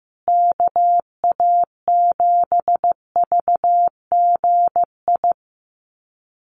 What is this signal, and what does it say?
Morse code, heard as a keyed tone, about 15 wpm, 700 Hz. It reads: KA7VGI
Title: USA Call Signs Repeated 2x 15wpm